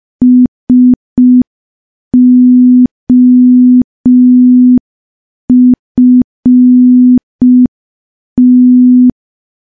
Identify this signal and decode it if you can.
Morse code, heard as a keyed tone, about 5 wpm, 250 Hz. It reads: SOFT
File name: morse / morse-training-05WPM